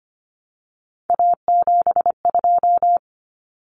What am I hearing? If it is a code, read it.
Morse code, heard as a keyed tone, about 25 wpm, 700 Hz. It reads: A72
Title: Common Cut Numbers - 3-Digit 25wpm